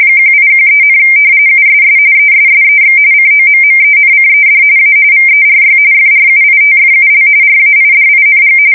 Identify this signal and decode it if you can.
RTTY, about 45.45 baud. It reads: 1813, USS ESSEX COMMENCED A VERY FRUITFUL RAIDING
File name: fsk170 / usnavy2